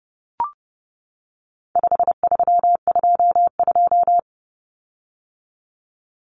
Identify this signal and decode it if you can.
Morse code, heard as a keyed tone, about 30 wpm, 700 Hz. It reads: 5322
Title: Common Cut Numbers - 4-Digit 30wpm